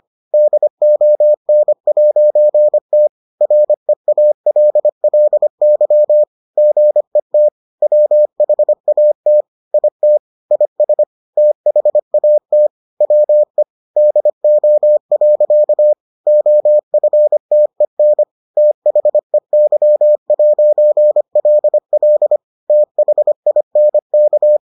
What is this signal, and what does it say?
Morse code, heard as a keyed tone, about 25 wpm, 600 Hz. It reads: DON'T REALLY GET WHAT IT IS THAT WE DO. OFTEN THEY'LL THINK